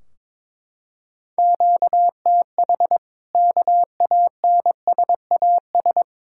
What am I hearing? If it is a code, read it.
Morse code, heard as a keyed tone, about 21 wpm, 700 Hz. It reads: QTH KANSAS
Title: Practice QSO - Exchange 1 - Beginning - 22wpm